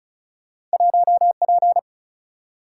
Morse code, heard as a keyed tone, about 36 wpm, 700 Hz. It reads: 1 P
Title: Single-Letter Single-Number 35wpm